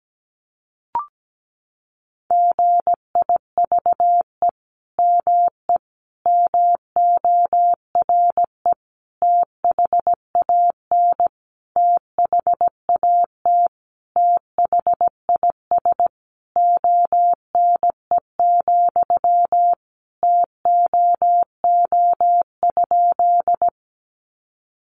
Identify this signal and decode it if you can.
Morse code, heard as a keyed tone, about 17 wpm, 700 Hz. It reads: GIVE ME MORE THAN THAT THIS ONE, TOO?
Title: Sentences from Top 200 Words 17wpm